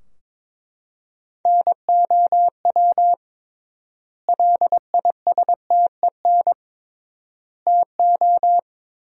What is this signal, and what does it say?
Morse code, heard as a keyed tone, about 22 wpm, 700 Hz. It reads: NOW LISTEN TO
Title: 3x Word-Spacing - Sentences from Top 400 Words - 22wpm